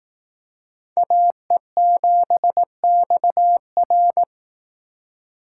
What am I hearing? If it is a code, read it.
Morse code, heard as a keyed tone, about 18 wpm, 700 Hz. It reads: AE7XR
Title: Speed Racing - USA 2x2 Call Signs 15wpm